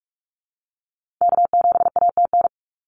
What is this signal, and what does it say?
Morse code, heard as a keyed tone, about 46 wpm, 700 Hz. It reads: K7ATN
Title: NA SOTA Activator Callsigns 45wpm